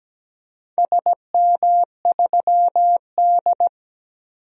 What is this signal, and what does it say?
Morse code, heard as a keyed tone, about 17 wpm, 700 Hz. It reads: SM3D
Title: DXCC-100 Call Signs - 4 Characters - 17wpm